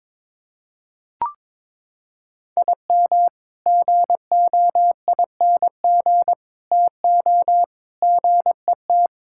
Morse code, heard as a keyed tone, about 21 wpm, 700 Hz. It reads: IM GOING TO GET
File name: practice / 22wpm / Sentences with 3-Word Phrases - 22wpm